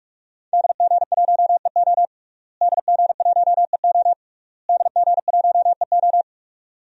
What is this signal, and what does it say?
Morse code, heard as a keyed tone, about 46 wpm, 700 Hz. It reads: DG1EO DG1EO DG1EO
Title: DXCC-100 Call Signs - 5 Characters - Repeated 3x - 45wpm